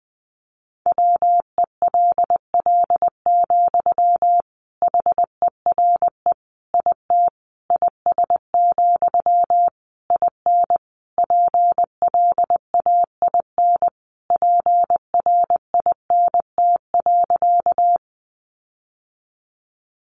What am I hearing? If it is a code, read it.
Morse code, heard as a keyed tone, about 20 wpm, 700 Hz. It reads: WELL, HERE IT IS, IN PLAIN PRINT.
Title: Sentences from Top 1400 Words 20wpm